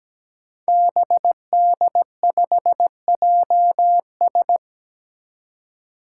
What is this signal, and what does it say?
Morse code, heard as a keyed tone, about 17 wpm, 700 Hz. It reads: BD5JS
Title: Five Challenging Letter-Number Combinations 17wpm